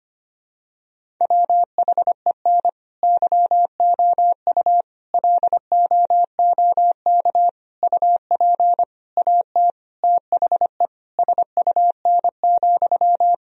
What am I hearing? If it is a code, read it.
Morse code, heard as a keyed tone, about 25 wpm, 700 Hz. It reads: WHEN YOU LOOK UP AT THE SUN,
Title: Sentences from Top 1000 Words - Review Part 1 - 25wpm